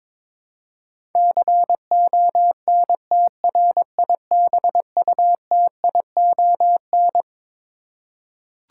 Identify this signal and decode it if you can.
Morse code, heard as a keyed tone, about 22 wpm, 700 Hz. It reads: CONTRIBUTION
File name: Twelve Letter Words - 22wpm